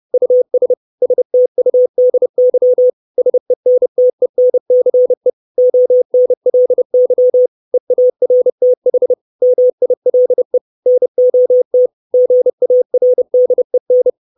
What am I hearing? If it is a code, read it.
Morse code, heard as a keyed tone, about 30 wpm, 500 Hz. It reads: US STUDY SENTENCE ONLY EARTH MILE NOT GARDEN
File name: 632 Words 30WPM